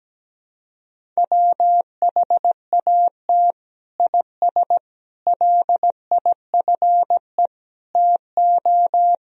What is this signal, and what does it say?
Morse code, heard as a keyed tone, about 17 wpm, 700 Hz. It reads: WHAT IS LIFE TO
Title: Sentences from Top 300 Words and Abbreviations 17wpm